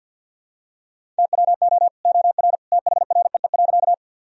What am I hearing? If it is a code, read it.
Morse code, heard as a keyed tone, about 51 wpm, 700 Hz. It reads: TWO OR THREE.
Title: Sentences from Top 1000 Words - Review Part 1 - 50wpm